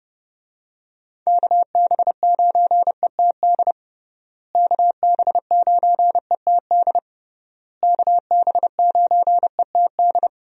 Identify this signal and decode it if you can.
Morse code, heard as a keyed tone, about 30 wpm, 700 Hz. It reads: KB9ETD KB9ETD KB9ETD
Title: USA Call Signs - 6 Characters - Repeated 3x - 30wpm